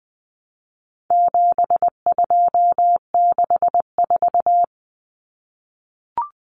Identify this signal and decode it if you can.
Morse code, heard as a keyed tone, about 20 wpm, 700 Hz. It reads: 7264
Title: Numbers - 4-Digit 20wpm